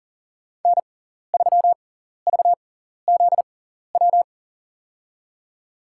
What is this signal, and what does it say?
Morse code, heard as a keyed tone, about 40 wpm, spaced as for 14 wpm, 700 Hz. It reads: N3VZW
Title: USA Call Signs - 5 Characters - 6x Character Spacing - 40wpm